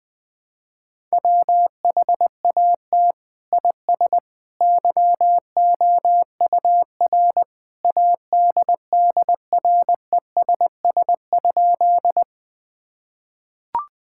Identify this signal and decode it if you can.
Morse code, heard as a keyed tone, about 20 wpm, 700 Hz. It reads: WHAT IS YOUR ADDRESS?
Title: Sentences from Top 1000 Words - Review Part 6 - 20wpm